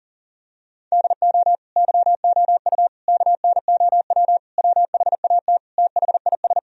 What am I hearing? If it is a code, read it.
Morse code, heard as a keyed tone, about 37 wpm, 700 Hz. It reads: DO YOU KNOW WHAT THIS
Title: Sentences from Top 1000 Words - Review Part 1 - 40wpm